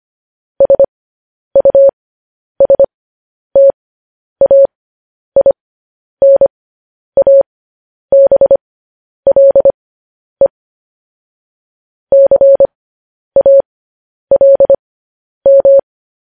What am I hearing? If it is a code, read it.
Morse code, heard as a keyed tone, about 25 wpm, spaced as for 10 wpm, 550 Hz. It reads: SUSTAINABLE CALM
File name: podcast / News headlines-1x-25x10